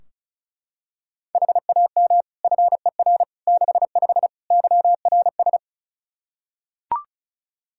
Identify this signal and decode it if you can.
Morse code, heard as a keyed tone, about 36 wpm, 700 Hz. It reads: HAM FER 65 YRS
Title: Practice QSO - Exchange 3 - More Info - 35wpm